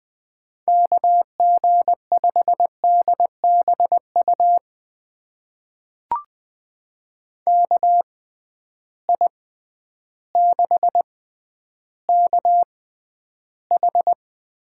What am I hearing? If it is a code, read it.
Morse code, heard as a keyed tone, about 20 wpm, spaced as for 11 wpm, 700 Hz. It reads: KG5DBU KI6KH
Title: USA Call Signs - 6 Characters - 6x Character Spacing - 20wpm